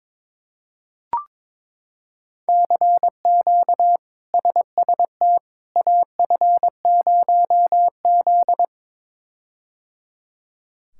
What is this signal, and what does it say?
Morse code, heard as a keyed tone, about 22 wpm, 700 Hz. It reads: CQ SST AF0Z
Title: K1USN SST QSO Practice - 22wpm